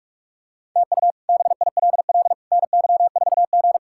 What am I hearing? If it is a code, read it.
Morse code, heard as a keyed tone, about 46 wpm, 700 Hz. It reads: TU BILL NY4G
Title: K1USN SST QSO Practice - 45wpm